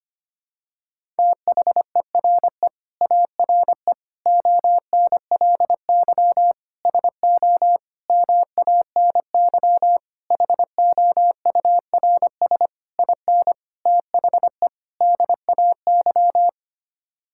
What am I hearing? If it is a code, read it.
Morse code, heard as a keyed tone, about 25 wpm, 700 Hz. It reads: THERE ARE ONLY SO MANY HOURS IN THE DAY
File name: Sets of 9 Words - 25wpm